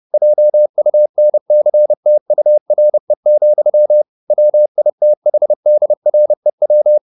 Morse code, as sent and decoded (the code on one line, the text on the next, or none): .--- ..- -. -.-. - ..- .-. --..-- / .-- .. - .... -.. .-. .--
JUNCTURE, WITHDREW